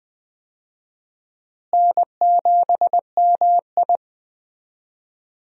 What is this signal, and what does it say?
Morse code, heard as a keyed tone, about 20 wpm, 700 Hz. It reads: N7MI
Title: USA Call Signs - 4 Characters - Repeated 2x - 20wpm